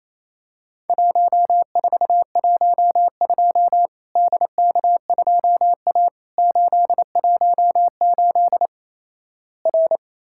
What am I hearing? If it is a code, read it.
Morse code, heard as a keyed tone, about 28 wpm, 700 Hz. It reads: 1412 DK2A 818 R
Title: Worked All Europe DX Contest - QTC Exchange - 28wpm